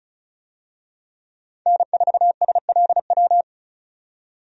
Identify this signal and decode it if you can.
Morse code, heard as a keyed tone, about 36 wpm, 700 Hz. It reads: N4SLW
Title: USA Call Signs - 5 Characters - 35wpm